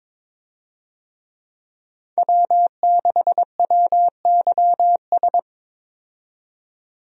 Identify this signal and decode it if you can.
Morse code, heard as a keyed tone, about 22 wpm, 700 Hz. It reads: W6WYS
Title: USA Call Signs - 5 Characters - 22wpm